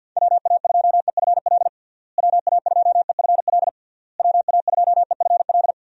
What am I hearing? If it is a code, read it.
Morse code, heard as a keyed tone, about 47 wpm, 700 Hz. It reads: WA2EFL WA2EFL WA2EFL
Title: USA Call Signs - 6 Characters - Repeated 3x - 50wpm